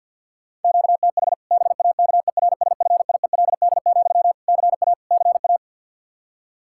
Morse code, heard as a keyed tone, about 45 wpm, 700 Hz. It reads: QTH BAKERSFIELD, CA CA
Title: Practice QSO - Exchange 1 - Beginning - 50wpm